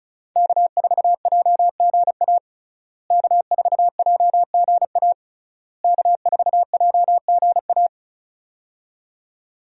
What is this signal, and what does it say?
Morse code, heard as a keyed tone, about 36 wpm, 700 Hz. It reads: K4JGA K4JGA K4JGA
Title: USA Call Signs - 5 Characters - Repeated 3x - 35wpm